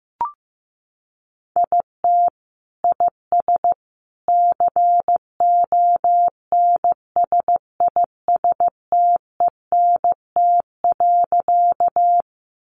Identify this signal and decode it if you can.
Morse code, heard as a keyed tone, about 15 wpm, 700 Hz. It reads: IT IS CONSISTENT.
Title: Sentences from Top 1500 Words 15wpm